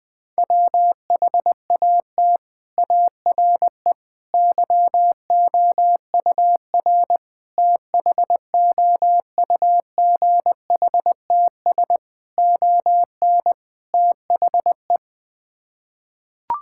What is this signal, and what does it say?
Morse code, heard as a keyed tone, about 20 wpm, 700 Hz. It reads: WHAT ARE YOUR THOUGHTS ON THE
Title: Sets of 6 Words - 20wpm